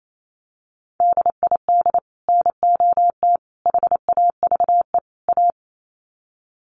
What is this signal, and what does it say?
Morse code, heard as a keyed tone, about 28 wpm, 700 Hz. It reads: DID NOT HAVE A
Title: Sets of 4 Words 28wpm